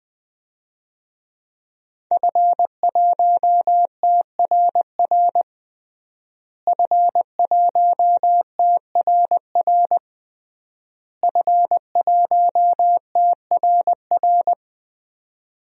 Morse code, heard as a keyed tone, about 20 wpm, 700 Hz. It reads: F1TRR F1TRR F1TRR
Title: DXCC-100 Call Signs - 5 Characters - Repeated 3x - 20wpm